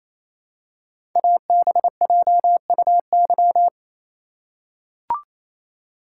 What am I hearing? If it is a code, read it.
Morse code, heard as a keyed tone, about 28 wpm, 700 Hz. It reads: ABJUY
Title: Code Groups - 5 Letters - 28wpm